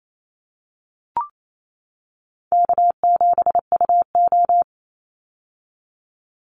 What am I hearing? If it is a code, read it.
Morse code, heard as a keyed tone, about 28 wpm, 700 Hz. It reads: K7UO
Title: USA Call Signs 28wpm